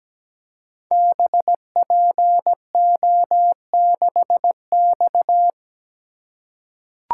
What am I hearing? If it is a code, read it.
Morse code, heard as a keyed tone, about 17 wpm, 700 Hz. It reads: BPO6X
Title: Code Groups - 5 Letter:Numbers - 17wpm